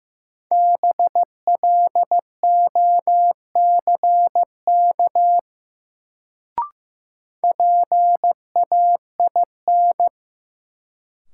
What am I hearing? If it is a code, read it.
Morse code, heard as a keyed tone, about 15 wpm, 700 Hz. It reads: BLOCK PAIN
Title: Sentences from Top 1600 Words - New Word Drill Alt - 15wpm